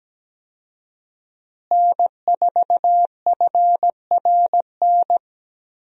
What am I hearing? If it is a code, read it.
Morse code, heard as a keyed tone, about 17 wpm, 700 Hz. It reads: N4FRN
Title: USA Call Signs - 5 Characters - Repeated 3x - 17wpm